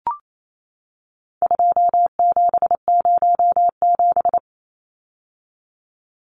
Morse code, heard as a keyed tone, about 28 wpm, 700 Hz. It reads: 2707
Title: Numbers - 4-Digit 28wpm